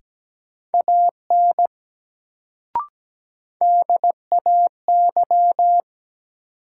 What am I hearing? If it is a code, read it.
Morse code, heard as a keyed tone, about 17 wpm, 700 Hz. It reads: AN DAY
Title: Sets of 2 Words - New Word Drill - 17wpm